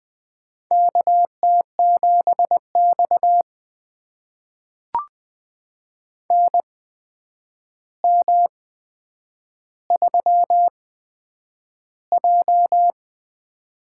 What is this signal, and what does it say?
Morse code, heard as a keyed tone, about 20 wpm, spaced as for 9 wpm, 700 Hz. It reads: KT7X NM3J
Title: USA Call Signs - 4 Characters - 8x Character Spacing - 20wpm